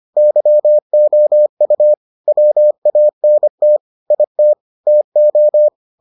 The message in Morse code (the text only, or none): YOU WANT IT TO